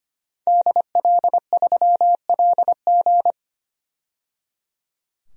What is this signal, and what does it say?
Morse code, heard as a keyed tone, about 25 wpm, 700 Hz. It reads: DL3LG
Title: DXCC-100 Call Signs - 5 Characters - 25wpm